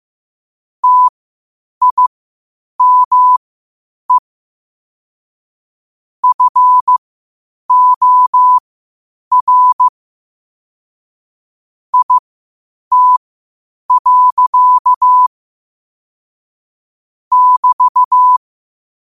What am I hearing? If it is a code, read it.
Morse code, heard as a keyed tone, about 15 wpm, spaced as for 8 wpm, 1000 Hz. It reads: TIME FOR IT. <BT>
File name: qotdfiles / 8wpm